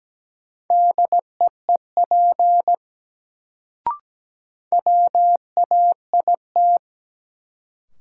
Morse code, heard as a keyed tone, about 17 wpm, 700 Hz. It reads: DEEP WAIT
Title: Sets of 5 Words Encore - New Word Drill - 17wpm